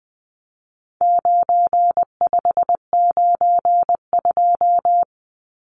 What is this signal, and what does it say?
Morse code, heard as a keyed tone, about 20 wpm, 700 Hz. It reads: 9592
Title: Numbers - 4-Digit 20wpm